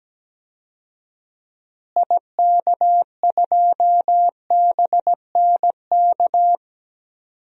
Morse code, heard as a keyed tone, about 17 wpm, 700 Hz. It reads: IK2BNK
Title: DXCC-100 Call Signs - 6 Characters - Repeated 2x - 17wpm